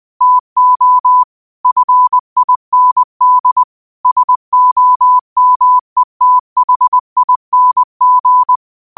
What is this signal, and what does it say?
Morse code, heard as a keyed tone, about 20 wpm, 1000 Hz. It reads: TO FIND SOMETHING
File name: qotdfiles / 20wpm